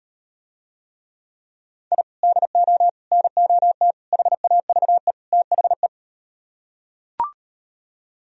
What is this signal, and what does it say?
Morse code, heard as a keyed tone, about 38 wpm, 700 Hz. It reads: I DO NOT HAVE THE